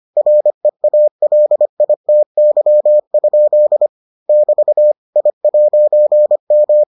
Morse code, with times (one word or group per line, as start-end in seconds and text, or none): 0.2-3.9 REALITY?
4.3-4.9 <BT>
5.2-6.8 I'M